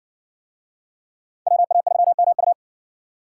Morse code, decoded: WA3MU

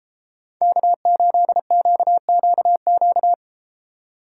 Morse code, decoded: K8QQQ

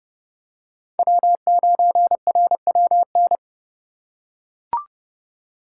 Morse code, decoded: W9RWN

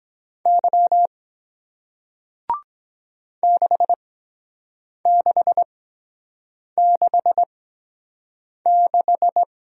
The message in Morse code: Y 6666

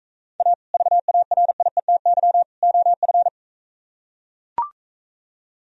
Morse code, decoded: A VARIETY OF

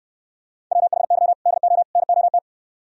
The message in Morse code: WHY DO NOT

